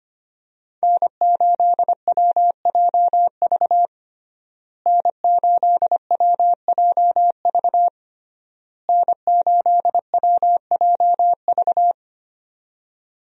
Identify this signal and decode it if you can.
Morse code, heard as a keyed tone, about 25 wpm, 700 Hz. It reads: N8WJV N8WJV N8WJV
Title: USA Call Signs - 5 Characters - Repeated 3x - 25wpm